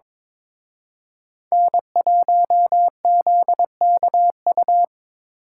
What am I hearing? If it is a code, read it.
Morse code, heard as a keyed tone, about 26 wpm, 700 Hz. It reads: N1ZKU
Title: Speed Racing - USA 1x3 Call Signs 22wpm